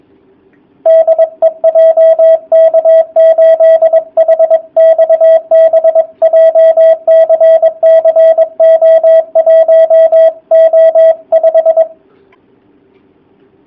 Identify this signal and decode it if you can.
Morse code, heard as a keyed tone, about 22 wpm, 660 Hz, with radio noise. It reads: DE JK8HXB JCC O1O5
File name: jh8cbh / 2019cw / sp23- w